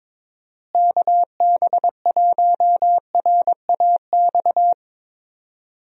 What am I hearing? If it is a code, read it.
Morse code, heard as a keyed tone, about 21 wpm, 700 Hz. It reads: KB1RAX
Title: USA Call Signs - 6 Characters - Repeated 2x - 22wpm